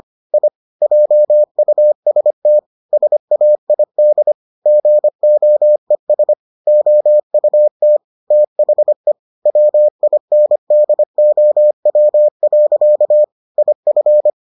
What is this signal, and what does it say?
Morse code, heard as a keyed tone, about 25 wpm, 600 Hz. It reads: I JUST SAID GOES OUT THE WINDOW. IF